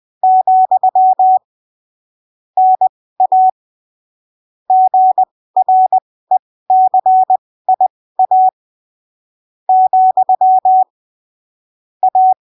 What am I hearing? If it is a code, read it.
Morse code, heard as a keyed tone, about 20 wpm, spaced as for 15 wpm, 750 Hz. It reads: , NA GRECIA , A
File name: arquivos / texto / texto 08 20ppm